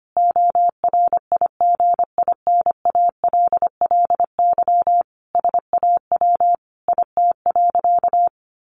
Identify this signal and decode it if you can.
Morse code, heard as a keyed tone, about 25 wpm, 700 Hz. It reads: ORIGINALLY SAW IT.
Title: Sentences from Top 1500 Words 25wpm